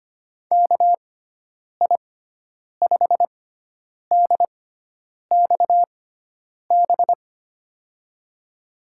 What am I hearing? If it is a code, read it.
Morse code, heard as a keyed tone, about 25 wpm, spaced as for 9 wpm, 700 Hz. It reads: KI5DXB
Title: USA Call Signs - 6 Characters - 6x Character Spacing - 25wpm